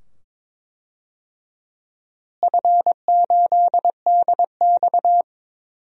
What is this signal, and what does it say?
Morse code, heard as a keyed tone, about 22 wpm, 700 Hz. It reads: F8DX
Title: DXCC-100 Call Signs - 4 Characters - Repeated 2x - 22wpm